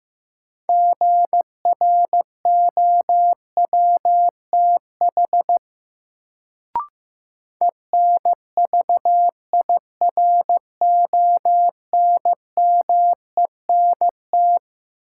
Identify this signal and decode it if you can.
Morse code, heard as a keyed tone, about 15 wpm, 700 Hz. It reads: GROWTH ENVIRONMENT
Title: Sentences from Top 1000 Words - Review Part 5 - New Word Drill - 15wpm